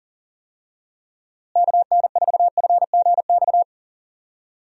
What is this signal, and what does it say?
Morse code, heard as a keyed tone, about 40 wpm, 700 Hz. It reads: KN4FGX
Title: USA Call Signs - 6 Characters - Repeated 2x - 40wpm